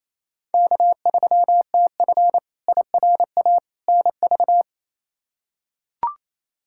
Morse code, heard as a keyed tone, about 28 wpm, 700 Hz. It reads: K3TF IRA NV
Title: K1USN SST QSO Practice - 28wpm